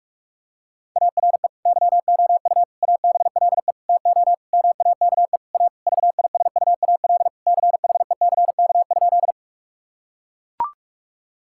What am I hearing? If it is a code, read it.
Morse code, heard as a keyed tone, about 44 wpm, 700 Hz. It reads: ARE YOU ABLE TO MAKE A VISUAL CHECK?
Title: Sentences from Top 1000 Words - Review Part 6 - 45wpm